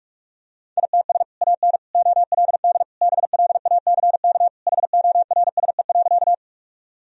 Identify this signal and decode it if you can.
Morse code, heard as a keyed tone, about 47 wpm, 700 Hz. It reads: ITS AN OLD BLACK HORSE.
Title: Sentences from Top 1000 Words - Review Part 2 - 45wpm